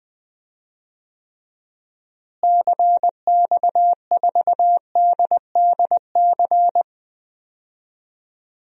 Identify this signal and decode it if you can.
Morse code, heard as a keyed tone, about 20 wpm, 700 Hz. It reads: CX4DDC
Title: DXCC-100 Call Signs - 6 Characters - Repeated 3x - 20wpm